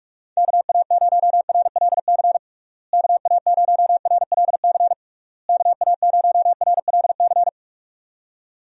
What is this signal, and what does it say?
Morse code, heard as a keyed tone, about 46 wpm, 700 Hz. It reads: KA0RLC KA0RLC KA0RLC
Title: USA Call Signs - 6 Characters - Repeated 3x - 45wpm